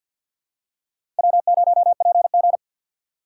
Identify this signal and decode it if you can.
Morse code, heard as a keyed tone, about 53 wpm, 700 Hz. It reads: W0PG
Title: USA Call Signs - 4 Characters - 2x Character Spacing - 50wpm